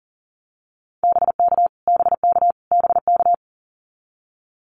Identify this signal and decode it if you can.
Morse code, heard as a keyed tone, about 40 wpm, 700 Hz. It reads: BK BK BK
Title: Abbreviations Repeated 3x - 40wpm